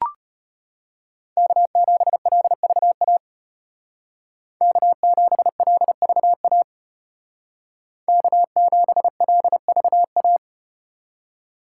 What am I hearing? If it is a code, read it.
Morse code, heard as a keyed tone, about 34 wpm, 700 Hz. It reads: K7LVA K7LVA K7LVA